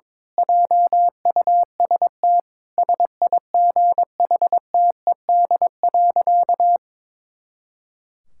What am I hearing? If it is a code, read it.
Morse code, heard as a keyed tone, about 22 wpm, 700 Hz. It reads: JUST SIGHTED.
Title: Pangram Challenge v2 - 22wpm